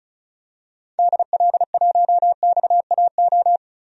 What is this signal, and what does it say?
Morse code, heard as a keyed tone, about 34 wpm, 700 Hz. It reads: DL1XAO
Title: DXCC-100 Call Signs - 6 Characters - Repeated 2x - 35wpm